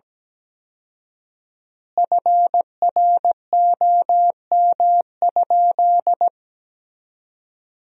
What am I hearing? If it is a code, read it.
Morse code, heard as a keyed tone, about 17 wpm, 700 Hz. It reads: FROM?
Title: 4x Word-Spacing - Sentences from Top 300 Words - 17wpm